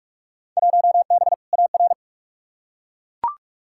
Morse code, heard as a keyed tone, about 41 wpm, 700 Hz. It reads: 1B AR
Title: Field Day Exchanges Encore - 45wpm